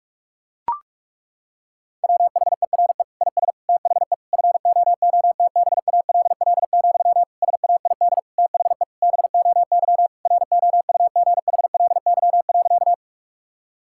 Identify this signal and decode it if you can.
Morse code, heard as a keyed tone, about 42 wpm, 700 Hz. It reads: WHERE IS THE FOOTBALL, SAID THE BOY ROUGHLY.
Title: Sentences from Top 1600 Words 45wpm